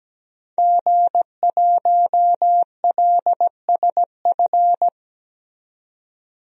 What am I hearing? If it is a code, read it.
Morse code, heard as a keyed tone, about 17 wpm, 700 Hz. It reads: G1LSF
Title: DXCC-100 Call Signs - 5 Characters - 17wpm